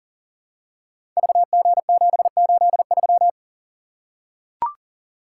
Morse code, heard as a keyed tone, about 38 wpm, 700 Hz. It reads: VG783
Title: Code Groups - 5 Letter:Numbers - 40wpm